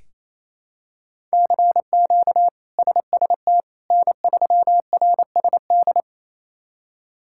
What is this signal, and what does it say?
Morse code, heard as a keyed tone, about 28 wpm, 700 Hz. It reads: CQ SST N3RSD
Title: K1USN SST QSO Practice - 28wpm